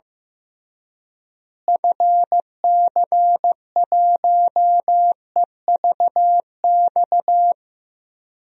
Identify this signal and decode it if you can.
Morse code, heard as a keyed tone, about 16 wpm, 700 Hz. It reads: FC1EVX
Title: DXCC-100 Call Signs - 6 Characters - Repeated 3x - 15wpm